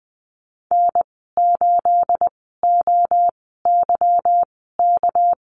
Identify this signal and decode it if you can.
Morse code, heard as a keyed tone, about 20 wpm, spaced as for 15 wpm, 700 Hz. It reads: N8OYK
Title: 219 Lesson 35g - Number 0 - Call Signs 20wpm